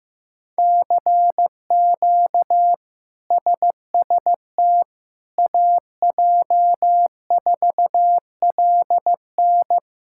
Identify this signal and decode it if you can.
Morse code, heard as a keyed tone, about 15 wpm, 700 Hz. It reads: CQ SST AJ4LN
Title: K1USN SST QSO Practice - 15wpm